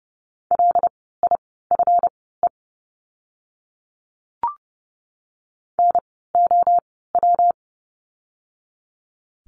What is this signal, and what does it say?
Morse code, heard as a keyed tone, about 30 wpm, spaced as for 17 wpm, 700 Hz. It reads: LIFE NOW
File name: course / 30:10wpm / 230 Lesson 37f - Letter x - Words 30wpm